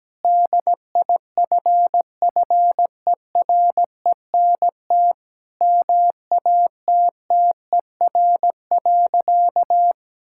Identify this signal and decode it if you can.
Morse code, heard as a keyed tone, about 17 wpm, 700 Hz. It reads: DIFFERENT MATTER.
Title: Sentences from Top 1600 Words 17wpm